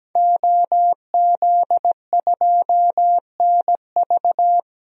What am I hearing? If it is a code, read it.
Morse code, heard as a keyed tone, about 17 wpm, 700 Hz. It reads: OZ2NV
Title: DXCC-100 Call Signs - 5 Characters - 17wpm